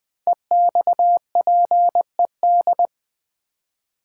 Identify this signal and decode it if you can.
Morse code, heard as a keyed tone, about 20 wpm, 700 Hz. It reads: EXPED
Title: Build A Word - 5 Letters At A Time - 10-Letter Words - 20wpm